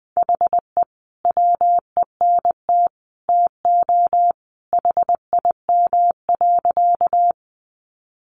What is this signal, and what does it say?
Morse code, heard as a keyed tone, about 20 wpm, 700 Hz. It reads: HE WENT TO HIM.
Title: Sentences from Top 200 Words 20wpm